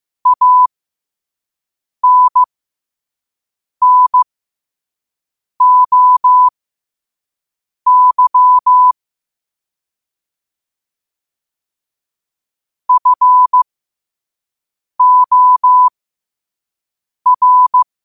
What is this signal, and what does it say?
Morse code, heard as a keyed tone, about 15 wpm, spaced as for 5 wpm, 1000 Hz. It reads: ANNOY FOR
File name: qotdfiles / 5wpm